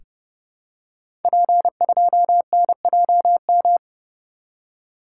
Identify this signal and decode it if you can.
Morse code, heard as a keyed tone, about 30 wpm, 700 Hz. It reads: P2NJM
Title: Code Groups - 5 Letter:Numbers - 30wpm